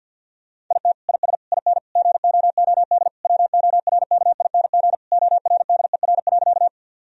Morse code, heard as a keyed tone, about 50 wpm, 700 Hz. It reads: IT IS IN GOOD WORKING ORDER.
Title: Sentences from Top 1000 Words - Review Part 4 - 50wpm